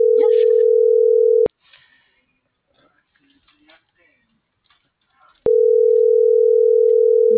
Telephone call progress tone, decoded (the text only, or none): ringback tone